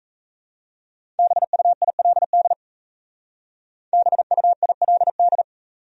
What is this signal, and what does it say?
Morse code, heard as a keyed tone, about 36 wpm, 700 Hz. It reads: BUILD BUILD